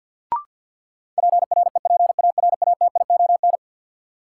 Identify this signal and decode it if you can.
Morse code, heard as a keyed tone, about 53 wpm, 700 Hz. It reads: PREPARATION
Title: Sentences from Top 1400 Words - New Word Drill - 50wpm